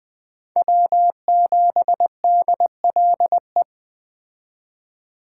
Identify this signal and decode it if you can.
Morse code, heard as a keyed tone, about 20 wpm, 700 Hz. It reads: W7DLE